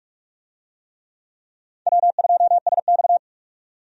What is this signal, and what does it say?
Morse code, heard as a keyed tone, about 44 wpm, 700 Hz. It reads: W2SX
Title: USA Call Signs - 4 Characters - 45wpm